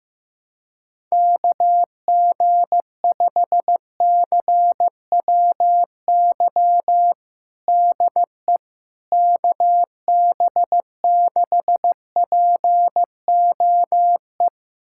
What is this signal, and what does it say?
Morse code, heard as a keyed tone, about 15 wpm, 700 Hz. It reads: KG5CWY DE KB6POE
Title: USA Callsign DE Callsign - 6 Characters - 15wpm